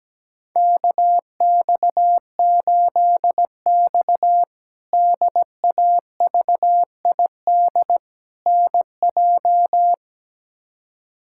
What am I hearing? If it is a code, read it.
Morse code, heard as a keyed tone, about 17 wpm, 700 Hz. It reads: KX8X DAVID NJ
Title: K1USN SST QSO Practice - 17wpm